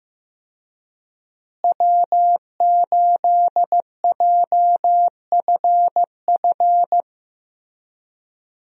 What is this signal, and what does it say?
Morse code, heard as a keyed tone, about 15 wpm, 700 Hz. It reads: W8JFF
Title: USA Call Signs - 5 Characters - Repeated 3x - 15wpm